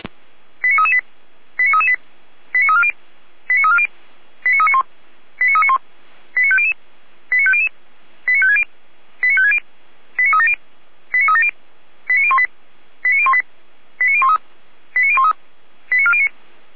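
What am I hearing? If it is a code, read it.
Five-tone selective call (ZVEI), heard as quick runs of five tones, tones of about 70 ms. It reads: 89338 89338 89340 89340 89391 89391 89500 89500 89560 89560 89370 89370 89018 89018 89013 89013 89409
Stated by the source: ZVEI five-tone selective call received on an SDR.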